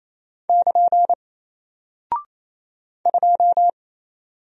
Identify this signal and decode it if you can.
Morse code, heard as a keyed tone, about 28 wpm, 700 Hz. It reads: <KN> 2
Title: Single Character (Letter:Number:Punctuation:Prosign) Alt 28wpm